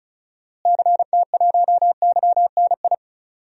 Morse code, heard as a keyed tone, about 36 wpm, 700 Hz. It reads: CT1YNI
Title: DXCC-100 Call Signs - 6 Characters - Repeated 2x - 35wpm